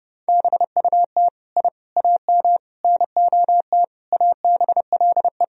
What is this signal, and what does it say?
Morse code, heard as a keyed tone, about 30 wpm, 700 Hz. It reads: BUT I AM NOT ABLE
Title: Sentences from Top 1000 Words - Review Part 2 - 30wpm